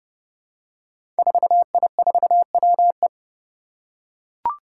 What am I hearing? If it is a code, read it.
Morse code, heard as a keyed tone, about 30 wpm, 700 Hz. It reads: 4I4WE
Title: Five Challenging Letter-Number Combinations 30wpm